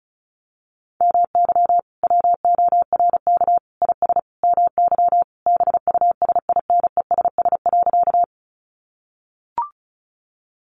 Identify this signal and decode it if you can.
Morse code, heard as a keyed tone, about 34 wpm, 700 Hz. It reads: MY WORK IS MY BUSINESS.
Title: Sentences from Top 600 Words 35wpm